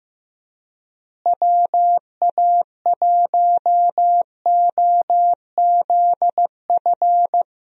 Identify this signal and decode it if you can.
Morse code, heard as a keyed tone, about 15 wpm, 700 Hz. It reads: WA1OZF
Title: USA Call Signs - 6 Characters - Repeated 3x - 15wpm